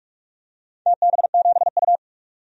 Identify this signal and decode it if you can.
Morse code, heard as a keyed tone, about 46 wpm, 700 Hz. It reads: TB7U
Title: DXCC-100 Call Signs - 4 Characters - 45wpm